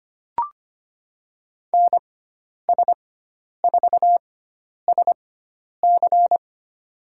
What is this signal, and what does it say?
Morse code, heard as a keyed tone, about 25 wpm, spaced as for 10 wpm, 700 Hz. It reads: NS4SC
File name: Lesson 16g - Letter c - Call Signs 25wpm